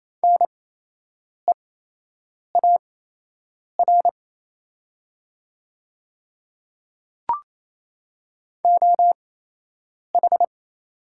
Morse code, heard as a keyed tone, about 28 wpm, spaced as for 8 wpm, 700 Hz. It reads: NEAR OH
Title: 8x Character Spacing - Top 500 Words - 28wpm